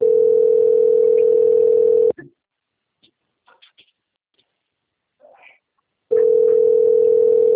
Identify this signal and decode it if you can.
Ringback tone (North America), 440 + 480 Hz.